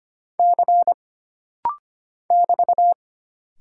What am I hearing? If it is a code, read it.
Morse code, heard as a keyed tone, about 25 wpm, 700 Hz. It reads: C <BT>
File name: Single Character (Letter:Number:Punctuation:Prosign) - Rapid-Fire - 25wpm